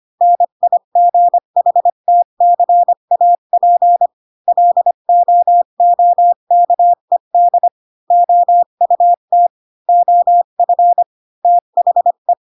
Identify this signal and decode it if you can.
Morse code, heard as a keyed tone, about 25 wpm, 700 Hz. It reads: NIGHTCAP LOOKED OUT OF THE
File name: Pinocchio / Pinocchio ch7 20 25 ik1hge